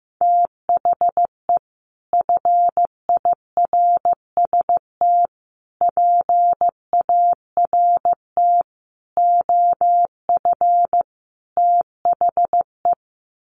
Morse code, heard as a keyed tone, about 15 wpm, 700 Hz. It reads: THE FIRST PART OF THE
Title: Sets of 5 Words Encore 15wpm